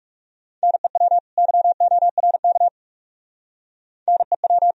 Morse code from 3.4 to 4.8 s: NEW

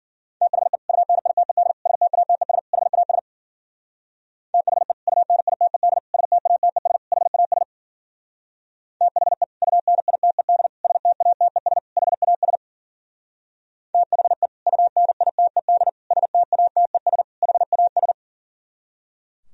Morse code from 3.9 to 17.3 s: THE UNITED STATES HAS THE UNITED STATES HAS THE UNITED STATES

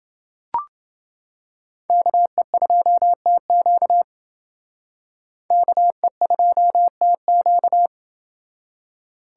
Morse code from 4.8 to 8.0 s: KE2TQ

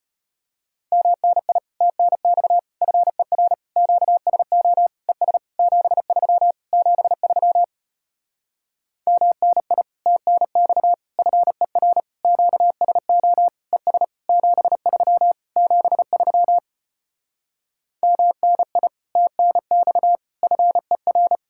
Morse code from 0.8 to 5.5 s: MNI TNX FER QSO ES

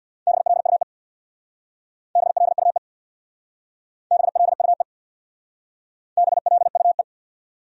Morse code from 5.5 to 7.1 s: BLUE

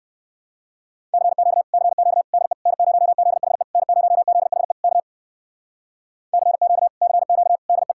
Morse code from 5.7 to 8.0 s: CQ CQ DE

